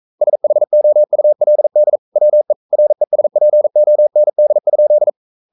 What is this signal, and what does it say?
Morse code, heard as a keyed tone, about 42 wpm, 600 Hz. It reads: SHOULD WE RESPOND?
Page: Romanen Never af Ken Follet på engelsk.